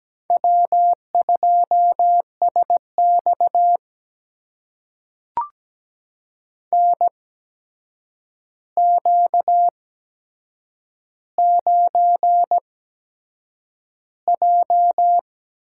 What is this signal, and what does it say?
Morse code, heard as a keyed tone, about 17 wpm, spaced as for 7 wpm, 700 Hz. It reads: W2SX NQ9J
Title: USA Call Signs - 8x Character Spacing - 17wpm